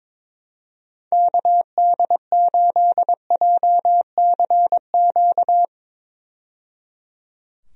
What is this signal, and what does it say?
Morse code, heard as a keyed tone, about 22 wpm, 700 Hz. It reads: KD8JCQ
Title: USA Call Signs - 6 Characters - 22wpm